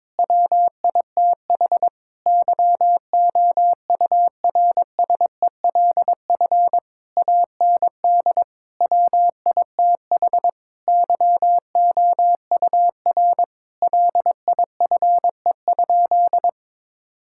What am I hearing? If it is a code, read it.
Morse code, heard as a keyed tone, about 22 wpm, 700 Hz. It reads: WITH YOURSELF AND WITH YOUR LIFE?
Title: Sentences from Top 1000 Words - Review Part 6 - 22wpm